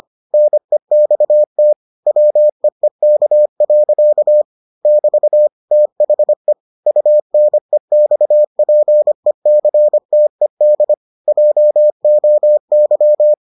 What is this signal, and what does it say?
Morse code, heard as a keyed tone, about 25 wpm, 600 Hz. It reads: NEXT WEEK. <BT> THE UNEXPECTED JOY